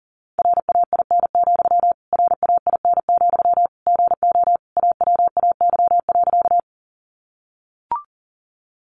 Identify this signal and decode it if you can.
Morse code, heard as a keyed tone, about 37 wpm, 700 Hz. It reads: RAIN, RAIN, GO AWAY.
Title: Sentences from Top 600 Words 40wpm